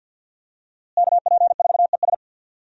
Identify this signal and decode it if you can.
Morse code, heard as a keyed tone, about 50 wpm, 700 Hz. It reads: KP4ES
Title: USA Call Signs - 5 Characters - Repeated 3x - 50wpm